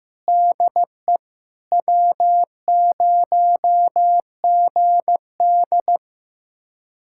Morse code, heard as a keyed tone, about 15 wpm, 700 Hz. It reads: DE W0GD
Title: USA Callsign DE Callsign - 4 Characters - 15wpm